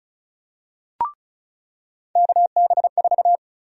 Morse code, heard as a keyed tone, about 34 wpm, 700 Hz. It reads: KB4
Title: USA Call Sign Prefixes - 35wpm